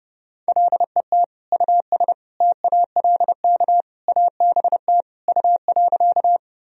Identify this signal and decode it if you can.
Morse code, heard as a keyed tone, about 30 wpm, 700 Hz. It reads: LET US TALK ABT U.
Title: Sentences from Top 400 Words and Abbreviations 30wpm